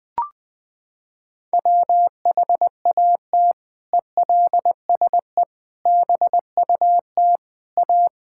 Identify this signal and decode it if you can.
Morse code, heard as a keyed tone, about 21 wpm, 700 Hz. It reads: WHAT ELSE BUT A
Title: Sentences from Top 1600 Words 20wpm